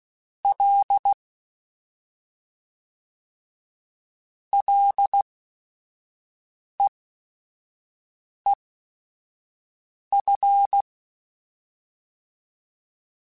Morse code, heard as a keyed tone, about 16 wpm, spaced as for 5 wpm, 800 Hz. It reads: L LEEF
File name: Ls / Single L m3s / Single L @5